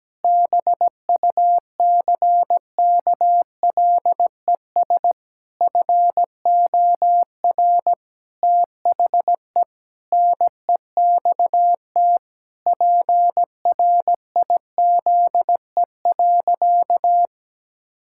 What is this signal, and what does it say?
Morse code, heard as a keyed tone, about 17 wpm, 700 Hz. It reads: BUCKLES FOR THE NEXT PRIZE.
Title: Pangram Challenge v2 - 17wpm